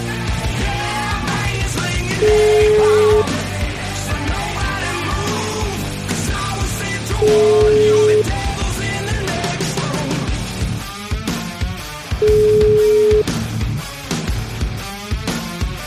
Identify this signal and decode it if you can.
Ringback tone (Europe, 425 Hz), with noise.